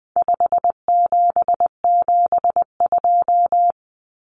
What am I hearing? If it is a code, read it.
Morse code, heard as a keyed tone, about 20 wpm, 700 Hz. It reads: 5772
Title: Common Cut Numbers - 4-Digit 20wpm